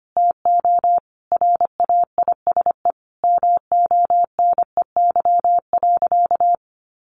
Morse code, heard as a keyed tone, about 25 wpm, 700 Hz. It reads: TO RAISE MONEY.
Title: Sentences from Top 1400 Words 25wpm